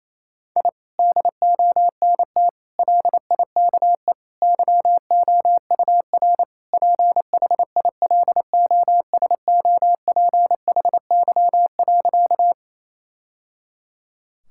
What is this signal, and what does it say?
Morse code, heard as a keyed tone, about 28 wpm, 700 Hz. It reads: I DONT LIKE YOUR PHILOSOPHY.
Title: Sentences from Top 1000 Words - Review Part 6 - 28wpm